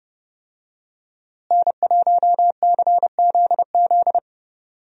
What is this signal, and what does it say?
Morse code, heard as a keyed tone, about 30 wpm, 700 Hz. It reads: N1CZZ
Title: USA Call Signs - 5 Characters - Repeated 3x - 30wpm